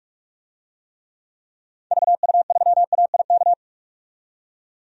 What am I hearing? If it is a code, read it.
Morse code, heard as a keyed tone, about 47 wpm, 700 Hz. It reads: VU3AIK
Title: DXCC-100 Call Signs - 6 Characters - 45wpm